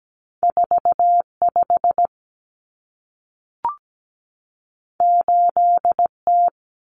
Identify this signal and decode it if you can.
Morse code, heard as a keyed tone, about 17 wpm, 700 Hz. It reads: 45 8T
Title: Common Cut Numbers - 2-Digit 17wpm